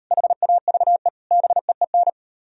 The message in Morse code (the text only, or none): HAVE BEEN